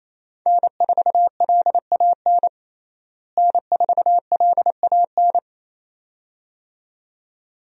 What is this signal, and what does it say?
Morse code, heard as a keyed tone, about 28 wpm, 700 Hz. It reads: N4LAN N4LAN
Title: USA Call Signs - 5 Characters - Repeated 2x - 28wpm